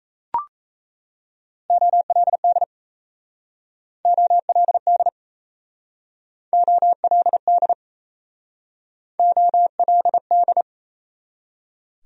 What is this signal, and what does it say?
Morse code, heard as a keyed tone, about 38 wpm, 700 Hz. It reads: OLD OLD OLD OLD